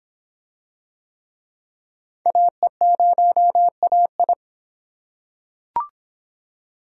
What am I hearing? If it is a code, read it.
Morse code, heard as a keyed tone, about 26 wpm, 700 Hz. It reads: AE0AI